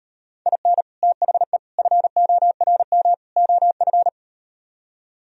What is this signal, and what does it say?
Morse code, heard as a keyed tone, about 39 wpm, 700 Hz. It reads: IN THE FORM OF